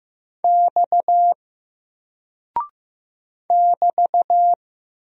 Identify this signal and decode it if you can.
Morse code, heard as a keyed tone, about 15 wpm, 700 Hz. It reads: X <BT>
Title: Single Character (Letter:Number:Punctuation:Prosign) 15wpm